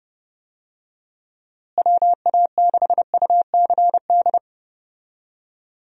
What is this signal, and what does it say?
Morse code, heard as a keyed tone, about 30 wpm, 700 Hz. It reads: WA6UCD
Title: USA Call Signs - 6 Characters - Repeated 3x - 30wpm